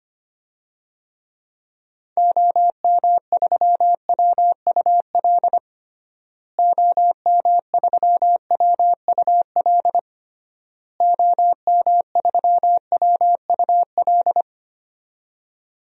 Morse code, heard as a keyed tone, about 25 wpm, 700 Hz. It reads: OM3WUL OM3WUL OM3WUL
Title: DXCC-100 Call Signs - 6 Characters - Repeated 3x - 25wpm